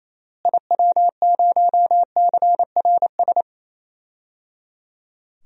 Morse code, heard as a keyed tone, about 28 wpm, 700 Hz. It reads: IW0CRS
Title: DXCC-100 Call Signs - 6 Characters - Repeated 3x - 28wpm